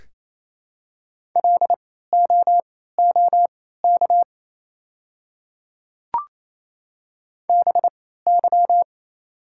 Morse code, heard as a keyed tone, about 28 wpm, spaced as for 16 wpm, 700 Hz. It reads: LOOK BY
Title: 3x Character Spacing - Top 300 Words - 28wpm